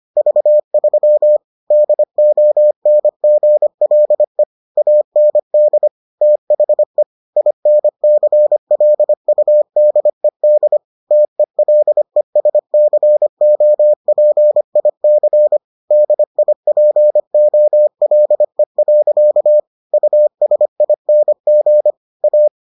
Morse code, heard as a keyed tone, about 25 wpm, 600 Hz. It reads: V3 DONGLE AND THE INCLUDED TELESCOPIC DIPOLE. USING A